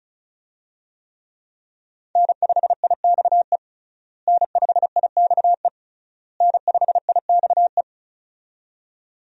Morse code, heard as a keyed tone, about 34 wpm, 700 Hz. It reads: N5IXE N5IXE N5IXE
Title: USA Call Signs - 5 Characters - Repeated 3x - 35wpm